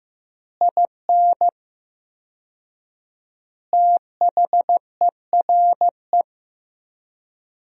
Morse code, heard as a keyed tone, about 15 wpm, 700 Hz. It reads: IN THERE
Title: 4x Word-Spacing - Sentences from Top 500 Words - 15wpm